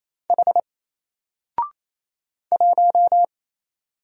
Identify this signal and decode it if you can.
Morse code, heard as a keyed tone, about 27 wpm, 700 Hz. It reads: H 1
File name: Single-Letter-Number-28wpm